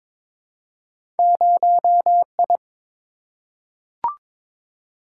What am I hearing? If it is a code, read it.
Morse code, heard as a keyed tone, about 22 wpm, 700 Hz. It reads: 0I
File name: practice / 22wpm / Two Letter-Number Combinations - 22wpm